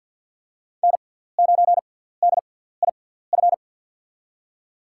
Morse code, heard as a keyed tone, about 49 wpm, spaced as for 17 wpm, 700 Hz. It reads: N9DIF
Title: USA Call Signs - 6x Character Spacing - 50wpm